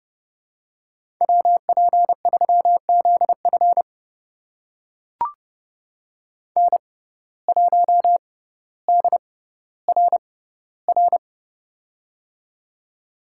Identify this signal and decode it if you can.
Morse code, heard as a keyed tone, about 30 wpm, spaced as for 15 wpm, 700 Hz. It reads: WP3ZF N1DRR
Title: USA Call Signs - 5 Characters - 6x Character Spacing - 30wpm